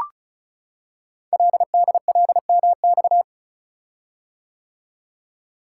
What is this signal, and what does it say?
Morse code, heard as a keyed tone, about 36 wpm, 700 Hz. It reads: LDLMX
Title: Code Groups - 5 Letters - 35wpm